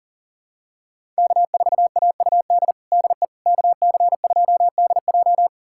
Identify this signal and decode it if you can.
Morse code, heard as a keyed tone, about 40 wpm, 700 Hz. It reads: K4AUD DE KC2DJ
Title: USA Callsign DE Callsign - 5 Characters - 40wpm